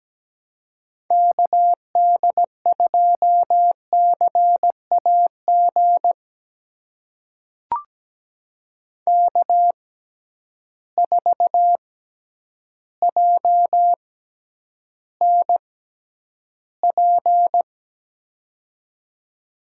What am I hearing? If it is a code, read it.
Morse code, heard as a keyed tone, about 17 wpm, spaced as for 9 wpm, 700 Hz. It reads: KD2CAG K4JNP
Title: USA Call Signs - 6x Character Spacing - 17wpm